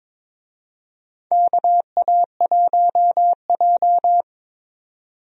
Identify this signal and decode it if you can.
Morse code, heard as a keyed tone, about 22 wpm, 700 Hz. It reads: KA1J
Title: USA Call Signs - 4 Characters - 22wpm